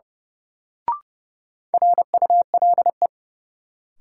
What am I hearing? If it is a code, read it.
Morse code, heard as a keyed tone, about 30 wpm, 700 Hz. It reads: RULE
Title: Sentences from Top 400 Words - New Word Drill - 30wpm